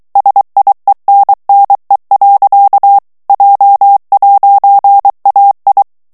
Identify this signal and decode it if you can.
Morse code, heard as a keyed tone, about 23 wpm, 800 Hz. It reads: SIENNE. J'AI
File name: CW / FICHIERS-MP3 / FRANCAIS TEXTE CLAIR 2 VITESSE 22 MOTS MINUTES